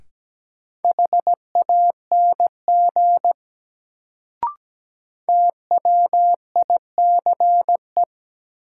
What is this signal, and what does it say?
Morse code, heard as a keyed tone, about 17 wpm, 700 Hz. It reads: HANG TWICE
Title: Sentences from Top 1600 Words - New Word Drill Alt - 17wpm